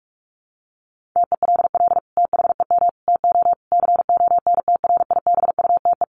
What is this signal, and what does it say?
Morse code, heard as a keyed tone, about 44 wpm, 700 Hz. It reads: TELL THEM TO CONTRIBUTE
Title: Sentences from Top 1400 Words 45wpm